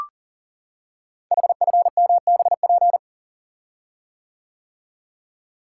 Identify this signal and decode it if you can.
Morse code, heard as a keyed tone, about 38 wpm, 690 Hz. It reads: HFMBP
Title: Code Groups - 5 Letter:Numbers - 40wpm